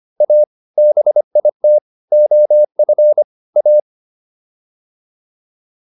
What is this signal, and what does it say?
Morse code, heard as a keyed tone, about 25 wpm, 600 Hz. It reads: A BIT OF A